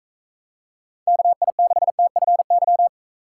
Morse code, heard as a keyed tone, about 45 wpm, 700 Hz. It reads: KI6TFY